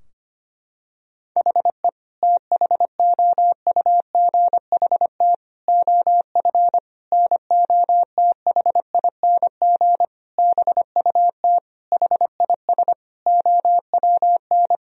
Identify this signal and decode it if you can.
Morse code, heard as a keyed tone, about 25 wpm, 700 Hz. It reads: HE THOUGHT OF NOTHING BUT HIS OWN
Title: Sentences from Top 1000 Words - Review Part 4 - 25wpm